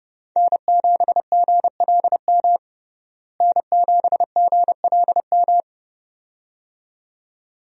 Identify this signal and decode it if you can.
Morse code, heard as a keyed tone, about 32 wpm, 700 Hz. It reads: N7GLM N7GLM
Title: USA Call Signs - 5 Characters - Repeated 2x - 30wpm